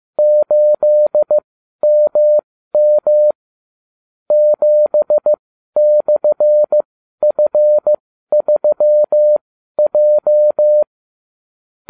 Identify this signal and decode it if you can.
Morse code, heard as a keyed tone, about 15 wpm, spaced as for 12 wpm, 600 Hz. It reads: 8MM 7/F3J